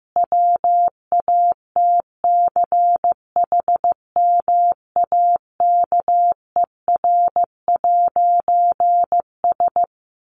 Morse code, heard as a keyed tone, about 15 wpm, 700 Hz. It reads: WATCHMAKER'S
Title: A Sound Of Thunder 15wpm